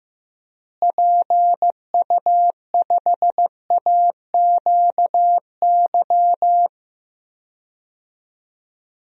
Morse code, heard as a keyed tone, about 15 wpm, 700 Hz. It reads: PU5AQY
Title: DXCC-100 Call Signs - 6 Characters - Repeated 2x - 15wpm